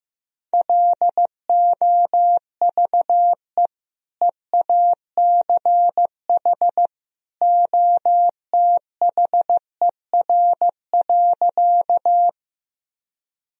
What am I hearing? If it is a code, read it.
Morse code, heard as a keyed tone, about 15 wpm, 700 Hz. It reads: LOVE EACH OTHER.
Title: Sentences from Top 1000 Words - Review Part 1 - 15wpm